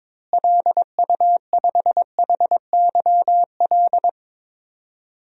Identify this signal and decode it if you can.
Morse code, heard as a keyed tone, about 22 wpm, 700 Hz. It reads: LU5HYL
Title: DXCC-100 Call Signs - 6 Characters - Repeated 2x - 22wpm